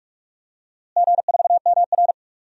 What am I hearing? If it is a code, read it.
Morse code, heard as a keyed tone, about 47 wpm, 700 Hz. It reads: G4MR
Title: DXCC-100 Call Signs - 4 Characters - Repeated 3x - 45wpm